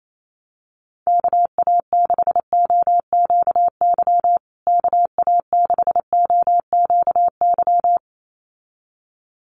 Morse code, heard as a keyed tone, about 28 wpm, 700 Hz. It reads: KA6OQY KA6OQY
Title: USA Call Signs Repeated 2x 28wpm